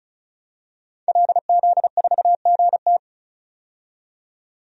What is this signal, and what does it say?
Morse code, heard as a keyed tone, about 34 wpm, 700 Hz. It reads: LZ4GT
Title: DXCC-100 Call Signs - 5 Characters - 35wpm